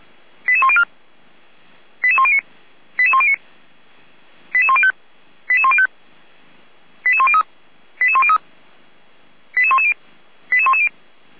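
Five-tone selective call (ZVEI), heard as quick runs of five tones, tones of about 70 ms. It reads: 88104 88109 88109 88105 88105 88103 88103 88110 88110
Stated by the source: ZVEI five-tone selective call 88070 received with an SDR.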